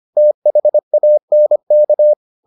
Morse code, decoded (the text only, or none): THANK